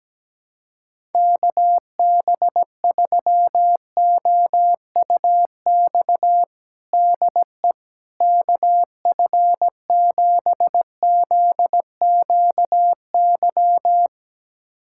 Morse code, heard as a keyed tone, about 17 wpm, 700 Hz. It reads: KB3OUX DE KF7ZQY
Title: USA Callsign DE Callsign - 6 Characters - 17wpm